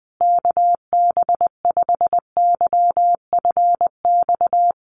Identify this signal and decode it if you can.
Morse code, heard as a keyed tone, about 20 wpm, 700 Hz. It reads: KB5YFX
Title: USA Call Signs Repeated 3x 20wpm